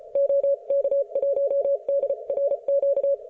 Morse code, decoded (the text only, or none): OK1DRQ